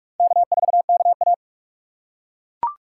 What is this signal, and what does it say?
Morse code, heard as a keyed tone, about 42 wpm, 700 Hz. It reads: K4KA
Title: USA Call Signs - 4 Characters - 2x Character Spacing - 45wpm